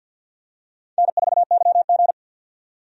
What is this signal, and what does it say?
Morse code, heard as a keyed tone, about 49 wpm, 700 Hz. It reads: N4YG
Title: USA Call Signs - 4 Characters - 50wpm